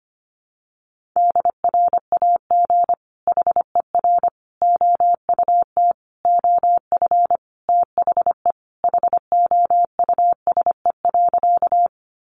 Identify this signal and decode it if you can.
Morse code, heard as a keyed tone, about 25 wpm, 700 Hz. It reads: DRAG HER OUT OF THE HOUSE.
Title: Sentences from Top 1500 Words 25wpm